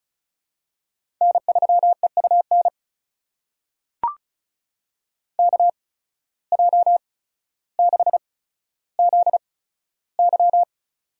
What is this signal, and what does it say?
Morse code, heard as a keyed tone, about 34 wpm, spaced as for 15 wpm, 700 Hz. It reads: N3EUN KJ6ZY
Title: USA Call Signs - 5 Characters - 8x Character Spacing - 35wpm